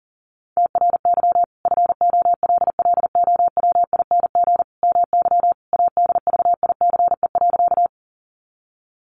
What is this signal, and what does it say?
Morse code, heard as a keyed tone, about 40 wpm, 700 Hz. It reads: TRY FOLLOWING MY ADVICE.
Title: Sentences from Top 1100 Words 40wpm